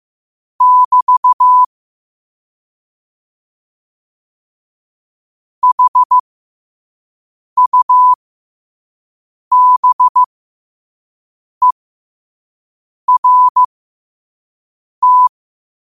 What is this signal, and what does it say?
Morse code, heard as a keyed tone, about 15 wpm, spaced as for 5 wpm, 1000 Hz. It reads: <BT> HUBERT